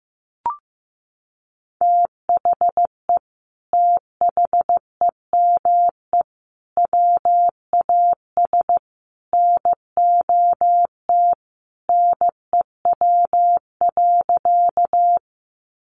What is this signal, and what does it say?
Morse code, heard as a keyed tone, about 15 wpm, 700 Hz. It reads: THE THEME WAS NOT NEW.
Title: Sentences from Top 1100 Words 15wpm